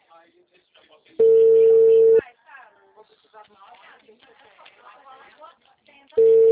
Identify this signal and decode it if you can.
Ringback tone (China, 450 Hz), with speech.